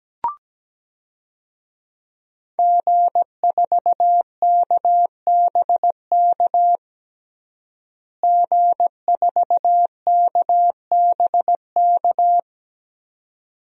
Morse code, heard as a keyed tone, about 17 wpm, 700 Hz. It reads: G4KBK G4KBK
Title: DXCC-100 Call Signs - 5 Characters - Repeated 2x - 17wpm